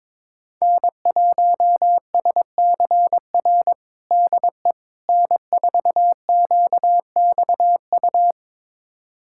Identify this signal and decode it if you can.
Morse code, heard as a keyed tone, about 21 wpm, 700 Hz. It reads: N1SCR DE N4QXU
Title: USA Callsign DE Callsign - 5 Characters - 22wpm